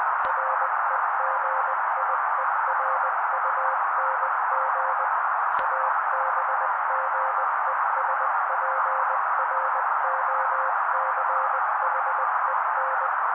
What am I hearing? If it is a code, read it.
Morse code, heard as a keyed tone, about 20 wpm, 530 Hz. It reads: REGIERUNG ABGESPROCHEN
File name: daily / 18wpm noise